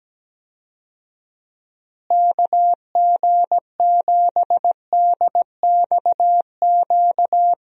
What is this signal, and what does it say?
Morse code, heard as a keyed tone, about 17 wpm, 700 Hz. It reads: KG7DXQ
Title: USA Call Signs - 6 Characters - Repeated 2x - 17wpm